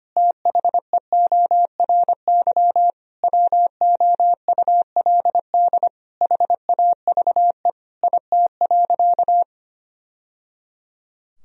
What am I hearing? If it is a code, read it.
Morse code, heard as a keyed tone, about 25 wpm, 700 Hz. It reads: THEORY WOULD HAVE IT.
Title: Sentences from Top 1000 Words - Review Part 3 - 25wpm